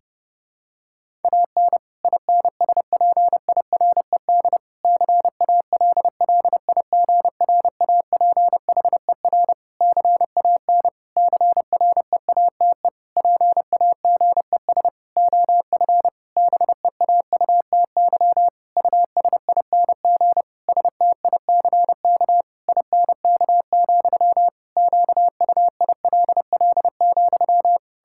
Morse code, heard as a keyed tone, about 30 wpm, 700 Hz. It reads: AN INSPIRED CALLIGRAPHER CAN CREATE PAGES OF BEAUTY USING STICK INK, QUILL,
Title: Pangram Challenge v2 - 30wpm